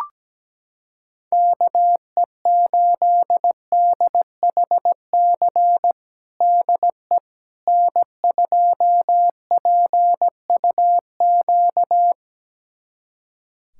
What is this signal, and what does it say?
Morse code, heard as a keyed tone, about 17 wpm, 700 Hz. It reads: KE8DHC DE N2PUQ
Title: USA Callsign DE Callsign 17wpm